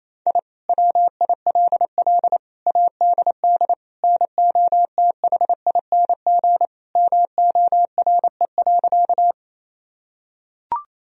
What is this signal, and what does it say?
Morse code, heard as a keyed tone, about 28 wpm, 700 Hz. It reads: I WILL ADD NOTHING MORE.
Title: Sentences from Top 1000 Words - Review Part 1 - 28wpm